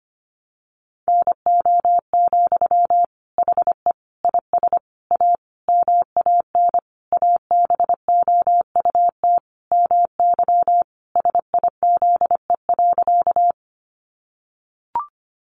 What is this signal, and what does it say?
Morse code, heard as a keyed tone, about 25 wpm, 700 Hz. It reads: NO, HE IS A MAN ABOUT MY SIZE.
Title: Sentences from Top 400 Words 25wpm